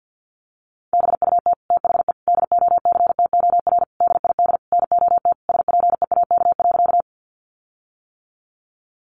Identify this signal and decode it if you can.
Morse code, heard as a keyed tone, about 50 wpm, 700 Hz. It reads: BUT THE DOCTOR DID NOT SPEAK.
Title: Sentences from Top 1500 Words 50wpm